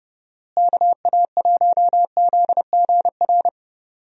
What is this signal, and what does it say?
Morse code, heard as a keyed tone, about 30 wpm, 700 Hz. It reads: KA1ZGR
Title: USA Call Signs - 6 Characters - Repeated 3x - 30wpm